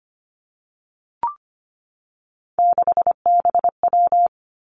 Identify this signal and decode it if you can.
Morse code, heard as a keyed tone, about 25 wpm, 700 Hz. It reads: 6BW
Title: Three Challenging Letter-Number Combinations 25wpm